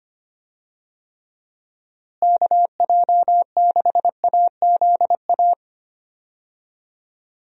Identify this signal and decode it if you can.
Morse code, heard as a keyed tone, about 25 wpm, 700 Hz. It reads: KJ6AZA
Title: USA Call Signs - 6 Characters - Repeated 3x - 25wpm